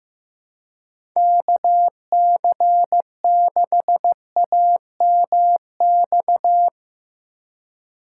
Morse code, heard as a keyed tone, about 15 wpm, 700 Hz. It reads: KC6AMX
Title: USA Call Signs - 6 Characters - 3x Character Spacing - 15wpm